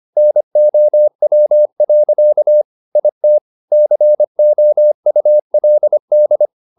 Morse code, heard as a keyed tone, about 25 wpm, 600 Hz. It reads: NOW. IT COULD
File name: the old man and the sea part 1 25wpm